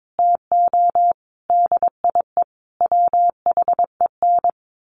Morse code, heard as a keyed tone, about 22 wpm, 700 Hz. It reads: TO DIE WHEN
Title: A Sound Of Thunder 22wpm